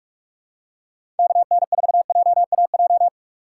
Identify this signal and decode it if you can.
Morse code, heard as a keyed tone, about 46 wpm, 700 Hz. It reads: KN4JAJ
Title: USA Call Signs - 6 Characters - 3x Character Spacing - 45wpm